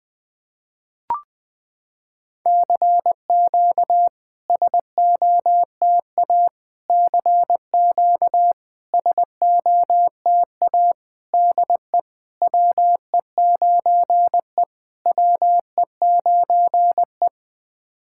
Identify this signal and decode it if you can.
Morse code, heard as a keyed tone, about 20 wpm, 700 Hz. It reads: CQ SOTA CQ SOTA DE WE9E WE9E
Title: SOTA QSO Practice 20wpm